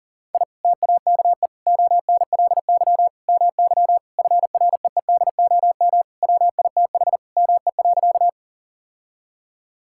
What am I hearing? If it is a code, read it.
Morse code, heard as a keyed tone, about 37 wpm, 700 Hz. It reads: I TAKE ONLY MY FREEDOM WITH ME.
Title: Sentences from Top 1000 Words - Review Part 5 - 40wpm